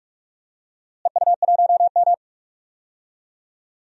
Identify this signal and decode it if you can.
Morse code, heard as a keyed tone, about 42 wpm, 700 Hz. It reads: EU1M
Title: DXCC-100 Call Signs - 4 Characters - Repeated 2x - 45wpm